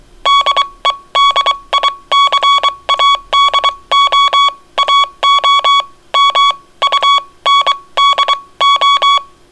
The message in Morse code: DEDICADO AO MUNDO